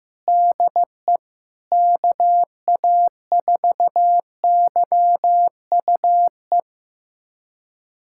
Morse code, heard as a keyed tone, about 15 wpm, 700 Hz. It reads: DE KA4YUE
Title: USA Callsign DE Callsign - 6 Characters - 15wpm